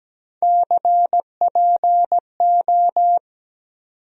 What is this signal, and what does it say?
Morse code, heard as a keyed tone, about 17 wpm, 700 Hz. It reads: CPO
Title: USA Call Sign Postfixes - 3 Letters 17wpm